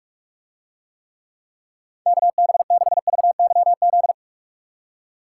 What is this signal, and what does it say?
Morse code, heard as a keyed tone, about 44 wpm, 700 Hz. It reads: KB6VYZ